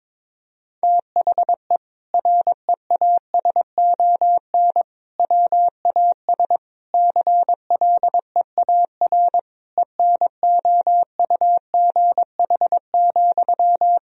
Morse code, heard as a keyed tone, about 22 wpm, 700 Hz. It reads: THE REASON WAS CLEAR ENOUGH,
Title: Sentences from Top 1000 Words - Review Part 2 - 22wpm